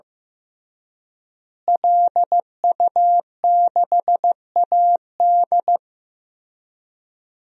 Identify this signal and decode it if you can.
Morse code, heard as a keyed tone, about 15 wpm, 700 Hz. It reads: LU6AD
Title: DXCC-100 Call Signs - 5 Characters - Repeated 3x - 15wpm